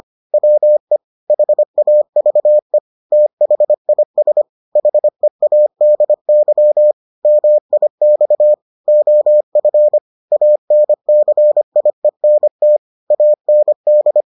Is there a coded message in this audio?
Yes: Morse code from 0.3 to 14.2 s, heard as a keyed tone, about 25 wpm, 600 Hz. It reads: WE HAVE THIS HEADY MIX OF ANCIENT AND